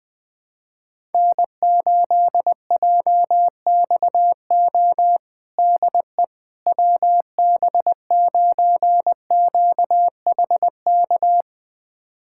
Morse code, heard as a keyed tone, about 20 wpm, 700 Hz. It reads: N8JXO DE WB9QHK
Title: USA Callsign DE Callsign 20wpm